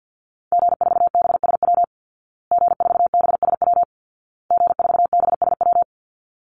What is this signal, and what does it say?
Morse code, heard as a keyed tone, about 50 wpm, 700 Hz. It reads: G4BSW G4BSW G4BSW
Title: Call Signs Repeated 2x - 50wpm